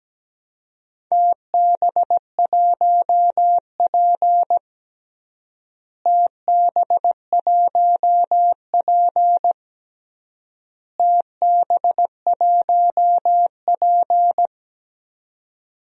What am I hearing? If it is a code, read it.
Morse code, heard as a keyed tone, about 17 wpm, 700 Hz. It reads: TB1P TB1P TB1P
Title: DXCC-100 Call Signs - 4 Characters - Repeated 3x - 17wpm